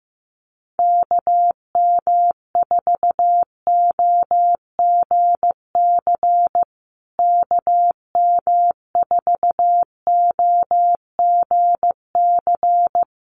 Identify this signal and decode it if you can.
Morse code, heard as a keyed tone, about 15 wpm, 700 Hz. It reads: KM4OGC KM4OGC
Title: USA Call Signs Repeated 3x 15wpm